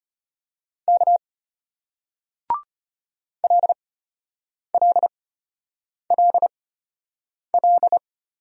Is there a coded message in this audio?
Yes: Morse code from 0.9 to 8.0 s, heard as a keyed tone, about 35 wpm, spaced as for 8 wpm, 700 Hz. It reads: K LLLL